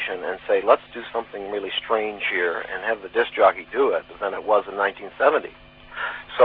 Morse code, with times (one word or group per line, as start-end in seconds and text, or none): none